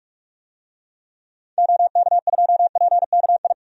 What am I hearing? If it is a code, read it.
Morse code, heard as a keyed tone, about 42 wpm, 700 Hz. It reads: OK2PKI
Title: DXCC-100 Call Signs - 6 Characters - 45wpm